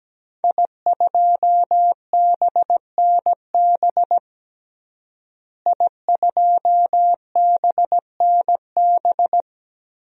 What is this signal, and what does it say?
Morse code, heard as a keyed tone, about 17 wpm, 700 Hz. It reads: I2BNB I2BNB
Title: DXCC-100 Call Signs - 5 Characters - Repeated 2x - 17wpm